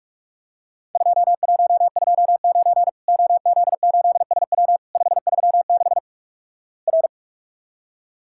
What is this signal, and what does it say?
Morse code, heard as a keyed tone, about 46 wpm, 700 Hz. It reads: 2129 OZ8SW 536 R
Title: Worked All Europe DX Contest - QTC Exchange - 45wpm